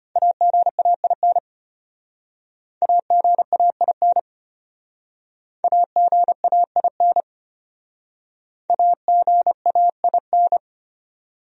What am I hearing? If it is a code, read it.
Morse code, heard as a keyed tone, about 31 wpm, 700 Hz. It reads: AGAIN AGAIN AGAIN AGAIN
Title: Speed-Racing Top 500 Words 25wpm